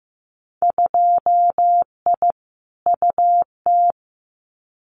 Morse code, heard as a keyed tone, about 15 wpm, 700 Hz. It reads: 2I UT
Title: Winter Field Day Exchanges 15wpm